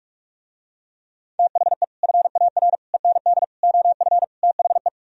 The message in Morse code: THE FAR END OF THE

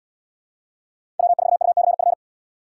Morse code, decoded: W4UCF

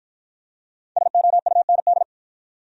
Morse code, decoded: SOUND

